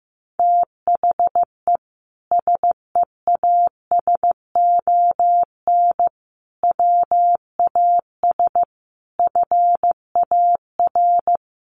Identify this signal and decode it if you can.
Morse code, heard as a keyed tone, about 14 wpm, 700 Hz. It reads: THE SEASON WAS FAR
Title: Sentences from Top 1400 Words 15wpm